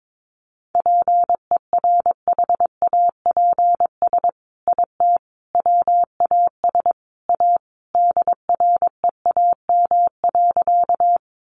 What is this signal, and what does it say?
Morse code, heard as a keyed tone, about 22 wpm, 700 Hz. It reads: PERHAPS IT WAS A DREAM.
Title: Sentences from Top 1500 Words 22wpm